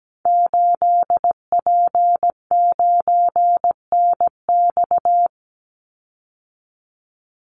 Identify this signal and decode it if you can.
Morse code, heard as a keyed tone, about 17 wpm, 700 Hz. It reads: 8P9NX
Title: Call Signs - 17wpm